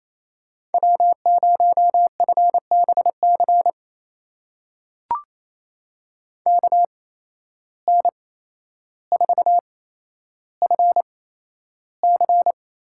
Morse code, heard as a keyed tone, about 28 wpm, spaced as for 12 wpm, 700 Hz. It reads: W0FBC KN4FC
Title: USA Call Signs - 5 Characters - 8x Character Spacing - 28wpm